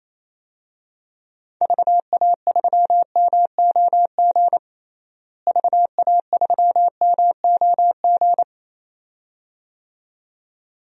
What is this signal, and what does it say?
Morse code, heard as a keyed tone, about 28 wpm, 700 Hz. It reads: VA3MOG VA3MOG
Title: DXCC-100 Call Signs - 6 Characters - Repeated 2x - 28wpm